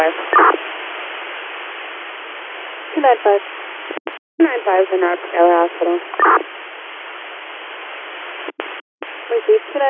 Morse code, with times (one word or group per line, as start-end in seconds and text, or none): none